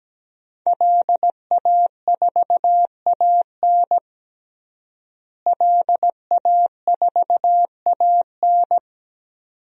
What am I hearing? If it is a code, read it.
Morse code, heard as a keyed tone, about 17 wpm, 700 Hz. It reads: LA4AN LA4AN
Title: DXCC-100 Call Signs - 5 Characters - Repeated 2x - 17wpm